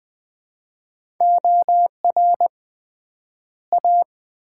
Morse code, heard as a keyed tone, about 20 wpm, 700 Hz. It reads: OR A
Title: 3x Word-Spacing - Sets of 2 Words - 20wpm